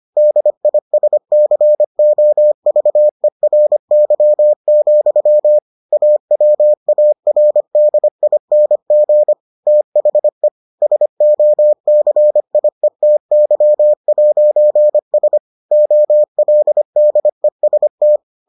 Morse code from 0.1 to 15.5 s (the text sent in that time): DISCOVERY, AWARDING THE SOCIETY'S